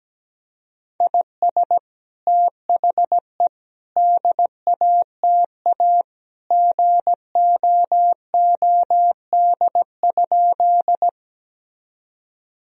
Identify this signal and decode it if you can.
Morse code, heard as a keyed tone, about 17 wpm, 700 Hz. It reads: IS THE DATA GOOD?
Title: Sentences from Top 1000 Words - Review Part 3 - 17wpm